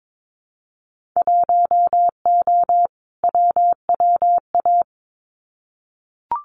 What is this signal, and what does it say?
Morse code, heard as a keyed tone, about 22 wpm, 700 Hz. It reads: 1O WWA
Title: Winter Field Day Exchanges 22wpm